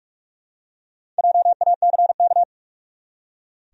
Morse code, heard as a keyed tone, about 44 wpm, 700 Hz. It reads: JACK
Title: USA Common Names - 45wpm